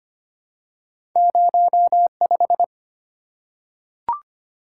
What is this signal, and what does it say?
Morse code, heard as a keyed tone, about 25 wpm, 690 Hz. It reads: 05
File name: Two Letter-Number Combinations - 25wpm